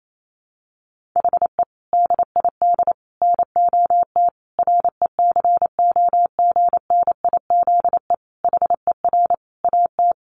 Morse code, heard as a keyed tone, about 28 wpm, 700 Hz. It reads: HE DID NOT RECOGNIZE HER AT
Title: Sentences from Top 1400 Words 28wpm